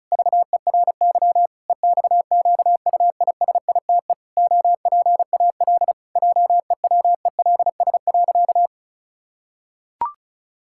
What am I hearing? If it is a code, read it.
Morse code, heard as a keyed tone, about 36 wpm, 700 Hz. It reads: VERY EXQUISITE OPAL JEWELS.
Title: Pangram Challenge v2 - 35wpm